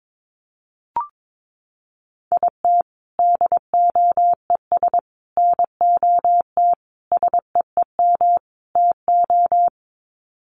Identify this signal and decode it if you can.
Morse code, heard as a keyed tone, about 22 wpm, 700 Hz. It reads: IT DOES NOT SEEM TO
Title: Sets of 5 Words Encore2 22wpm